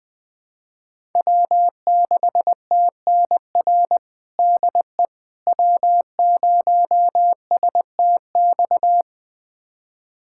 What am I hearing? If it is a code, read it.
Morse code, heard as a keyed tone, about 21 wpm, 700 Hz. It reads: W6TNR DE W0STX
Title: USA Callsign DE Callsign - 5 Characters - 20wpm